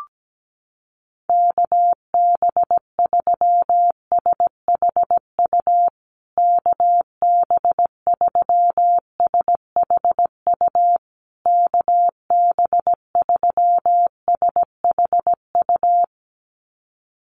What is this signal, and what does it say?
Morse code, heard as a keyed tone, about 17 wpm, 700 Hz. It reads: KB3SHU KB3SHU KB3SHU
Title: USA Call Signs Repeated 3x 17wpm